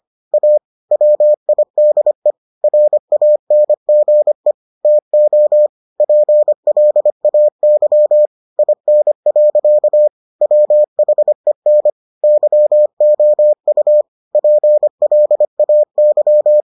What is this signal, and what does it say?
Morse code, heard as a keyed tone, about 25 wpm, 600 Hz. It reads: A WIDE RANGE TO PLAY IN. WHEN YOU PLAY